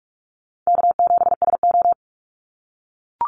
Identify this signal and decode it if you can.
Morse code, heard as a keyed tone, about 44 wpm, 700 Hz. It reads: K7SO
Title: NA SOTA Activator Callsigns 45wpm